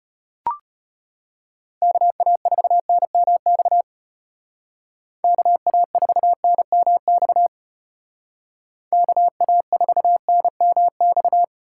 Morse code, heard as a keyed tone, about 33 wpm, 700 Hz. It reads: KA4NMX KA4NMX KA4NMX